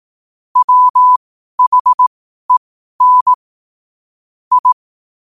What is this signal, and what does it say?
Morse code, heard as a keyed tone, about 18 wpm, spaced as for 12 wpm, 1000 Hz. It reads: WHEN I
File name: qotdfiles / 12wpm